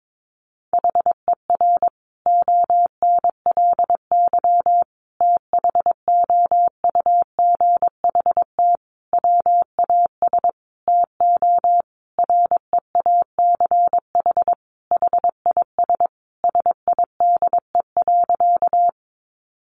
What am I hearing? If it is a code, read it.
Morse code, heard as a keyed tone, about 22 wpm, 700 Hz. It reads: HER ONLY THOUGHT WAS TO REACH HIS SIDE.
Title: Sentences from Top 400 Words 22wpm